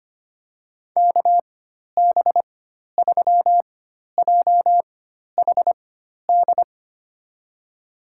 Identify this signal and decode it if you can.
Morse code, heard as a keyed tone, about 25 wpm, spaced as for 12 wpm, 700 Hz. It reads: KB3JHD
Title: USA Call Signs - 6 Characters - 4x Character Spacing - 25wpm